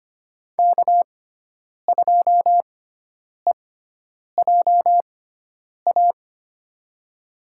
Morse code, heard as a keyed tone, about 25 wpm, spaced as for 9 wpm, 690 Hz. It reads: K2EJA
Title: USA Call Signs - 5 Characters - 6x Character Spacing - 25wpm